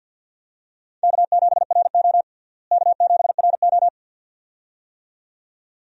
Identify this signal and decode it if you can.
Morse code, heard as a keyed tone, about 52 wpm, 700 Hz. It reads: K7RO K7RO
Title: USA Call Signs - 4 Characters - Repeated 2x - 50wpm